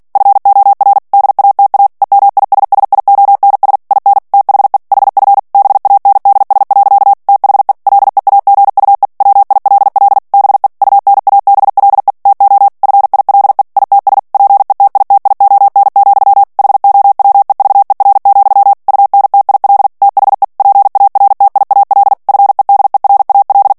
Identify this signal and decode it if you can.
Morse code, heard as a keyed tone, about 48 wpm, 800 Hz. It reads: FOR DATA EMISSIONS IN THE HF BANDS. THE LEAGUE WILL BE UNABLE TO FILE ITS PETITION, HOWEVER, UNTIL THE PARTIAL FEDERAL
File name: ANGLAIS texte clair 3 vitesse 45 mots minute